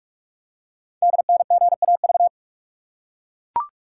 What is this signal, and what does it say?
Morse code, heard as a keyed tone, about 44 wpm, 700 Hz. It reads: DNGAV
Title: Code Groups - 5 Letters - 45wpm